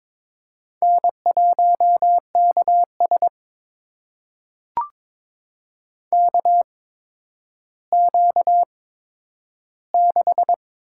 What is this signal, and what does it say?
Morse code, heard as a keyed tone, about 22 wpm, spaced as for 11 wpm, 700 Hz. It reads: N1KS KQ6
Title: USA Call Signs - 4 Characters - 8x Character Spacing - 22wpm